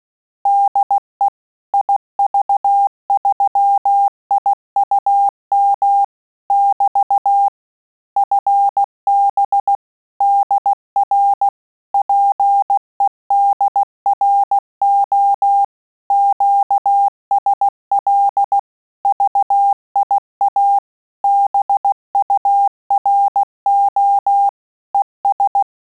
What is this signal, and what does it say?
Morse code, heard as a keyed tone, about 16 wpm, 800 Hz. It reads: DE IV3IUM <BT> FB DR PEDRO QSL VIA BURO ES
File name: Telegraphy-group / QSO CW 15 PPM